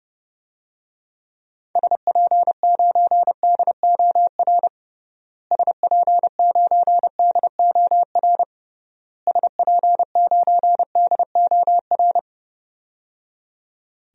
Morse code, heard as a keyed tone, about 30 wpm, 700 Hz. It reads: SP9DOR SP9DOR SP9DOR
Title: DXCC-100 Call Signs - 6 Characters - Repeated 3x - 30wpm